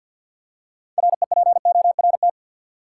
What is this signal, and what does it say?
Morse code, heard as a keyed tone, about 48 wpm, 700 Hz. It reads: REPORT
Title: Six Letter Words - 50wpm